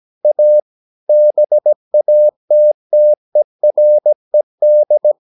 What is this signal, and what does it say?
Morse code, heard as a keyed tone, about 17 wpm, 600 Hz. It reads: A BATTERED
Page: Romanen Never af Ken Follet på engelsk.